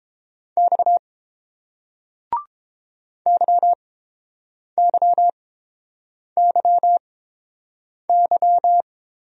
Morse code, heard as a keyed tone, about 31 wpm, spaced as for 7 wpm, 690 Hz. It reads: X YYYY